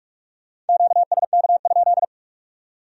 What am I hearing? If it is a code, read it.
Morse code, heard as a keyed tone, about 54 wpm, 700 Hz. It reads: QSK?
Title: Speed-Racing Q-Codes - 45wpm